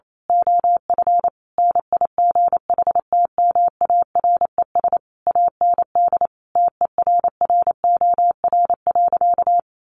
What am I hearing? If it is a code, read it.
Morse code, heard as a keyed tone, about 28 wpm, 700 Hz. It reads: OF NIGHTMARES AND TERROR.
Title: A Sound Of Thunder 28wpm